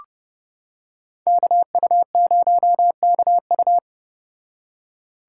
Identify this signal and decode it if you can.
Morse code, heard as a keyed tone, about 29 wpm, 700 Hz. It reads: KU0KU